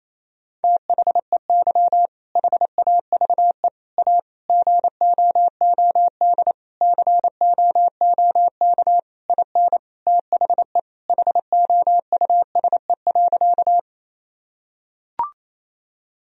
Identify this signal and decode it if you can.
Morse code, heard as a keyed tone, about 28 wpm, 700 Hz. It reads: THEY HAVE A GOOD COOK IN THE HOUSE.
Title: Sentences from Top 1000 Words - Review Part 4 - 28wpm